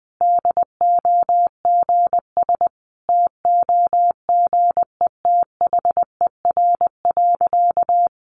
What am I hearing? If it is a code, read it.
Morse code, heard as a keyed tone, about 20 wpm, 700 Hz. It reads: DOGS TOGETHER.
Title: Easy Phrases 20wpm